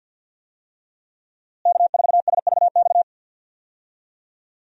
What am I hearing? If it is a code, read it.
Morse code, heard as a keyed tone, about 47 wpm, 700 Hz. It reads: K4SVX
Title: USA Call Signs - 5 Characters - Repeated 3x - 50wpm